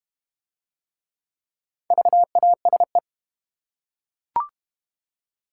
Morse code, heard as a keyed tone, about 31 wpm, 700 Hz. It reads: VASE